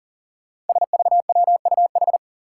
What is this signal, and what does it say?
Morse code, heard as a keyed tone, about 40 wpm, 700 Hz. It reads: SVWUH
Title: Five Challenging Letter-Number Combinations 40wpm